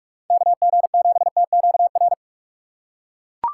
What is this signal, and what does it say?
Morse code, heard as a keyed tone, about 41 wpm, 700 Hz. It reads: KG7TQR
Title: USA Call Signs - 2x Character Spacing - 45wpm